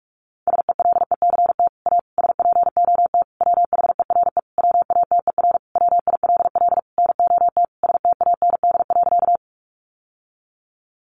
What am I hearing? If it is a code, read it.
Morse code, heard as a keyed tone, about 44 wpm, 700 Hz. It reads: SELECT A SPOT WHERE WATER WILL NOT STAND.
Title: Sentences from Top 1300 Words 45wpm